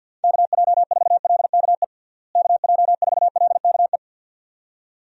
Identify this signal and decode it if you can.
Morse code, heard as a keyed tone, about 51 wpm, 700 Hz. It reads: KJ4LKE KJ4LKE
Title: USA Call Signs - 6 Characters - Repeated 2x - 50wpm